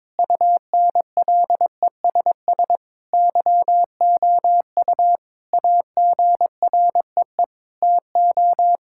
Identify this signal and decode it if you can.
Morse code, heard as a keyed tone, about 22 wpm, 700 Hz. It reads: UNLESS YOU AGREE TO
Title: Sentences from Top 1000 Words - Review Part 6 - 22wpm